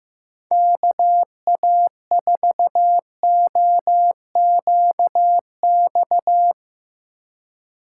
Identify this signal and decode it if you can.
Morse code, heard as a keyed tone, about 15 wpm, 700 Hz. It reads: KA4OQX
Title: USA Call Signs - 6 Characters - 4x Character Spacing - 15wpm